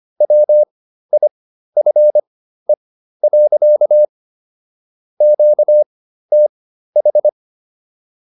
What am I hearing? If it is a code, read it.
Morse code, heard as a keyed tone, about 25 wpm, spaced as for 13 wpm, 600 Hz. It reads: WIFE. QTH